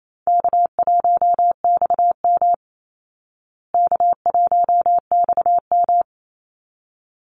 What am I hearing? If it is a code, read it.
Morse code, heard as a keyed tone, about 28 wpm, 700 Hz. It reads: K1XM K1XM
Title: Easy Call Signs Repeated 2x - 28wpm